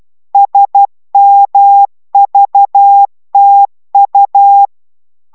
Morse code, heard as a keyed tone, about 12 wpm, 800 Hz. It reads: SMVTU
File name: taviro tanulas / cw 69 05 10